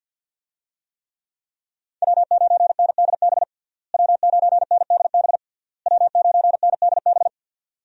Morse code, heard as a keyed tone, about 47 wpm, 700 Hz. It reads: W9NDB W9NDB W9NDB
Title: USA Call Signs - 5 Characters - Repeated 3x - 50wpm